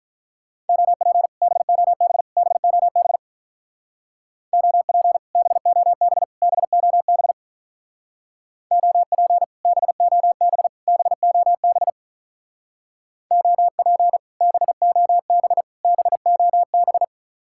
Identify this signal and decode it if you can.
Morse code, heard as a keyed tone, about 45 wpm, 700 Hz. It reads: OP BOB BOB OP BOB BOB OP BOB BOB OP BOB BOB